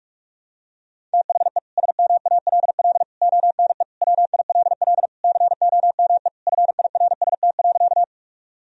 Morse code, heard as a keyed tone, about 41 wpm, 700 Hz. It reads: THE SMALL ONE WILL COME FIRST.
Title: Sentences from Top 1000 Words - Review Part 1 - 45wpm